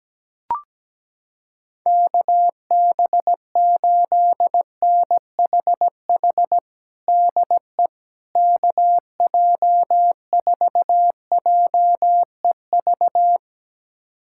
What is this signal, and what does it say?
Morse code, heard as a keyed tone, about 17 wpm, 700 Hz. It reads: KB8NHH DE KJ4JEV
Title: USA Callsign DE Callsign 17wpm